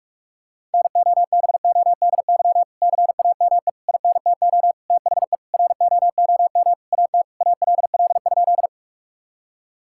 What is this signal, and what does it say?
Morse code, heard as a keyed tone, about 41 wpm, 700 Hz. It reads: NOBODY CAME INTO THE ROOM AT ALL?
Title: Sentences from Top 1600 Words 45wpm